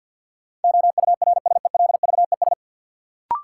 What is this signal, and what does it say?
Morse code, heard as a keyed tone, about 52 wpm, 700 Hz. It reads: OURSELVES
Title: Sentences from Top 1600 Words - New Word Drill - 50wpm